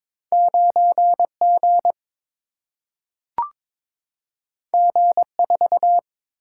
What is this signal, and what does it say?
Morse code, heard as a keyed tone, about 22 wpm, 700 Hz. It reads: 9 G G 4
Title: Single-Letter Single-Number 22wpm